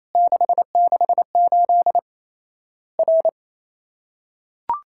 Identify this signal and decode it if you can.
Morse code, heard as a keyed tone, about 28 wpm, 700 Hz. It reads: 668 R
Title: Worked All Europe DX Contest - QTC Exchange - 28wpm